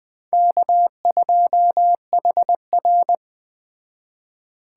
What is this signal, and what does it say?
Morse code, heard as a keyed tone, about 21 wpm, 700 Hz. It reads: K2HR
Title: Speed Racing - USA 1x2 Call Signs 17wpm